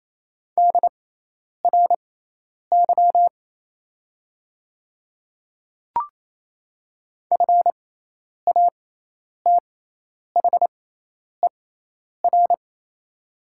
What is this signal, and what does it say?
Morse code, heard as a keyed tone, about 28 wpm, spaced as for 10 wpm, 700 Hz. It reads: DRY FATHER
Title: 6x Character Spacing - Top 500 Words - 28wpm